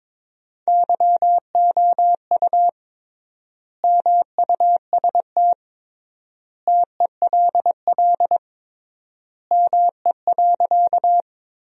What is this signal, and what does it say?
Morse code, heard as a keyed tone, about 22 wpm, 700 Hz. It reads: YOU MUST TELL ME.
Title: 3x Word-Spacing - Sentences from Top 200 Words - 22wpm